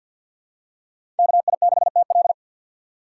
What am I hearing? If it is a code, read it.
Morse code, heard as a keyed tone, about 52 wpm, 700 Hz. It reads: KI6TL
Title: USA Call Signs - 5 Characters - 2x Character Spacing - 50wpm